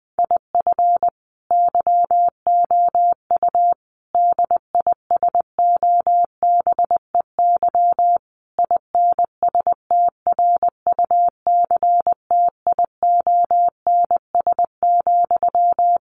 Morse code, heard as a keyed tone, about 20 wpm, 700 Hz. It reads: IF YOU DISOBEY INSTRUCTIONS,
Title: A Sound Of Thunder 20wpm